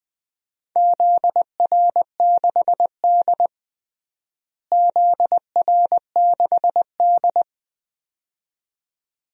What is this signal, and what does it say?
Morse code, heard as a keyed tone, about 20 wpm, 700 Hz. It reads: ZR6D ZR6D
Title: DXCC-100 Call Signs - 4 Characters - Repeated 2x - 20wpm